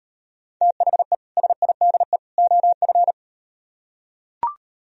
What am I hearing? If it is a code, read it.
Morse code, heard as a keyed tone, about 38 wpm, 700 Hz. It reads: THE SIDE OF